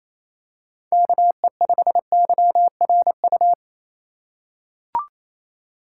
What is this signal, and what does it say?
Morse code, heard as a keyed tone, about 28 wpm, 700 Hz. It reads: KE5YRU
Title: USA Call Signs - 6 Characters - 3x Character Spacing - 28wpm